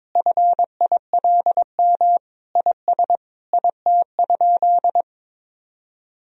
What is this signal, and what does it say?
Morse code, heard as a keyed tone, about 22 wpm, 700 Hz. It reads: FILM IS IT?
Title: Sentences from Top 1000 Words - Review Part 3 - 22wpm